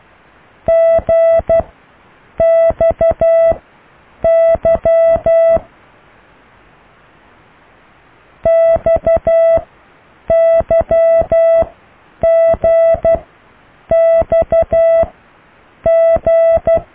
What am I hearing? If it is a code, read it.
Morse code, heard as a keyed tone, about 12 wpm, spaced as for 8 wpm, 660 Hz. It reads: GXY XYGXG